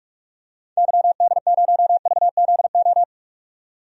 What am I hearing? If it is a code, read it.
Morse code, heard as a keyed tone, about 46 wpm, 700 Hz. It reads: YD0VZO
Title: DXCC-100 Call Signs - 6 Characters - 45wpm